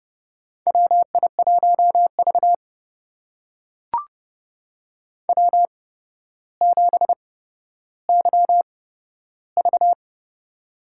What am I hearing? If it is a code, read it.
Morse code, heard as a keyed tone, about 30 wpm, spaced as for 13 wpm, 700 Hz. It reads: WI1V W7YV
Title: USA Call Signs - 4 Characters - 8x Character Spacing - 30wpm